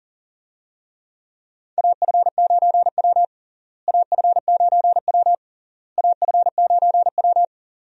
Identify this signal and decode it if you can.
Morse code, heard as a keyed tone, about 40 wpm, 700 Hz. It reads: AF9W AF9W AF9W
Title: USA Call Signs - 4 Characters - Repeated 3x - 40wpm